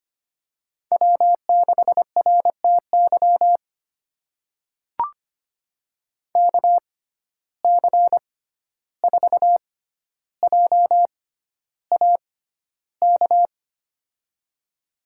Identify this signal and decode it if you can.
Morse code, heard as a keyed tone, about 25 wpm, spaced as for 12 wpm, 700 Hz. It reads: W6RTY KC4JAK
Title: USA Call Signs - 6x Character Spacing - 25wpm